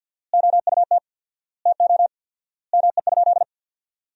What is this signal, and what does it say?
Morse code, heard as a keyed tone, about 50 wpm, 700 Hz. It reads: OUT TO ME?
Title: 4x Word-Spacing - Sentences from Top 200 Words - 50wpm